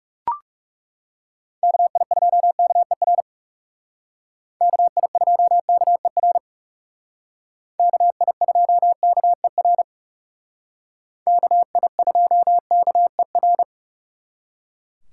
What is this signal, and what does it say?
Morse code, heard as a keyed tone, about 39 wpm, 700 Hz. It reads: KI2KER KI2KER KI2KER KI2KER